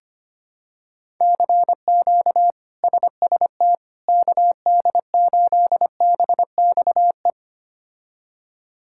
Morse code, heard as a keyed tone, about 25 wpm, 700 Hz. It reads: CQ SST KD8BXE
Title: K1USN SST QSO Practice - 25wpm